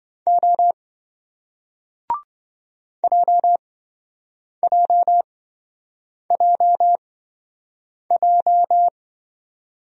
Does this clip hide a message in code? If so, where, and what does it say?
Yes: Morse code from 0.3 to 8.9 s, heard as a keyed tone, about 26 wpm, spaced as for 7 wpm, 700 Hz. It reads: O JJJJ